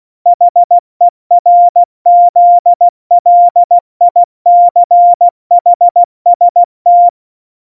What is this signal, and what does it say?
Morse code, heard as a keyed tone, about 16 wpm, 700 Hz. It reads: HERZLICHST
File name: Morsekurs / Witze / Witz 24. September 2019, 70 F80